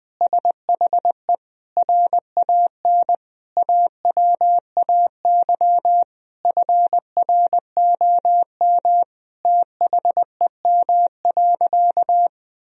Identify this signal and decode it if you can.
Morse code, heard as a keyed tone, about 20 wpm, 700 Hz. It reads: SHE RAN AWAY FROM THEM.
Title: Sentences from Top 1000 Words - Review Part 3 - 20wpm